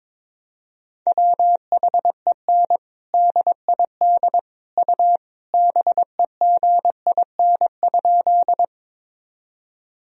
Morse code, heard as a keyed tone, about 22 wpm, 700 Hz. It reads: WHEN DID U BEGIN?
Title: Sentences from Top 300 Words and Abbreviations 22wpm